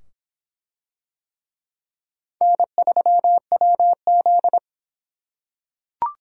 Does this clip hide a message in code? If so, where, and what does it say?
Yes: Morse code from 2.4 to 4.6 s, heard as a keyed tone, about 25 wpm, 700 Hz. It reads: N3WZ